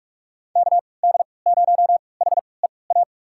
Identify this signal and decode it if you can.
Morse code, heard as a keyed tone, about 44 wpm, spaced as for 25 wpm, 690 Hz. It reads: KD0HEA
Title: USA Call Signs - 6 Characters - 3x Character Spacing - 45wpm